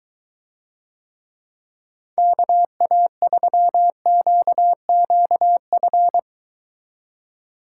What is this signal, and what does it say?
Morse code, heard as a keyed tone, about 22 wpm, 700 Hz. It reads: KA3QQF